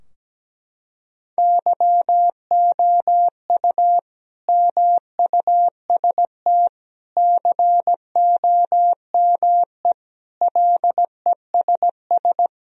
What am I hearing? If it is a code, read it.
Morse code, heard as a keyed tone, about 17 wpm, 700 Hz. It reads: YOU MUST COME LESS
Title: Sentences from Top 1000 Words - Review Part 2 - 17wpm